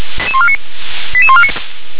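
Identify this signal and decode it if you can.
Five-tone selective call (ZVEI), heard as quick runs of five tones, tones of about 70 ms. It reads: 88149
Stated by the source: ZVEI five-tone selective call 88126 received by SDR.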